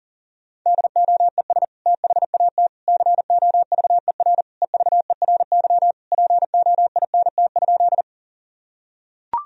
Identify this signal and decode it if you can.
Morse code, heard as a keyed tone, about 40 wpm, 700 Hz. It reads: DOES THAT COVER EVERY POINT?
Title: Sentences from Top 1000 Words - Review Part 1 - 40wpm